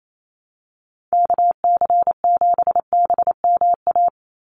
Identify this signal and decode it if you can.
Morse code, heard as a keyed tone, about 28 wpm, 700 Hz. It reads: KC7BMA
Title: USA Call Signs 28wpm